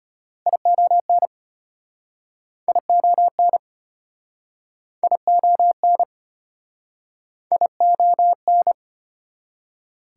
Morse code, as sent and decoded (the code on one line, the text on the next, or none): .. --- -. / .. --- -. / .. --- -. / .. --- -.
ION ION ION ION